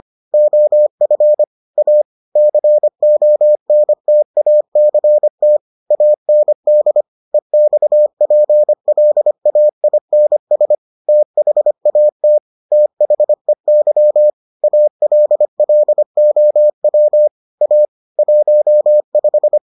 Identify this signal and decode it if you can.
Morse code, heard as a keyed tone, about 25 wpm, 600 Hz. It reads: OF A CONTACT AND EXPLAINS THAT THEY ALLOW A 15